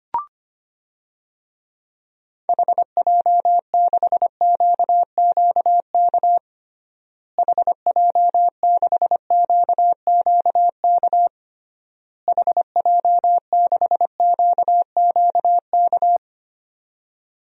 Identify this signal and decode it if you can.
Morse code, heard as a keyed tone, about 25 wpm, 700 Hz. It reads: HJ6QQK HJ6QQK HJ6QQK
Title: DXCC-100 Call Signs - 6 Characters - Repeated 3x - 25wpm